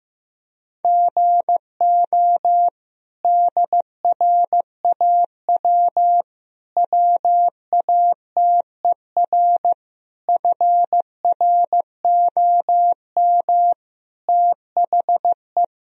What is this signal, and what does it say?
Morse code, heard as a keyed tone, about 15 wpm, 700 Hz. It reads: GO DRAW WATER FROM THE
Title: Sentences from Top 1000 Words - Review Part 1 - 15wpm